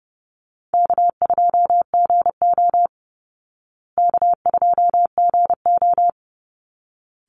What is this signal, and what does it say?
Morse code, heard as a keyed tone, about 30 wpm, 700 Hz. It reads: K2GO K2GO
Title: Call Signs Repeated 3x - 30wpm